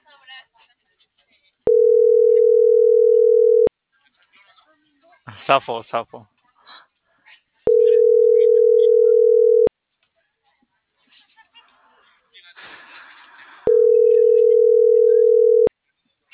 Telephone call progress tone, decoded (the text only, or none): ringback tone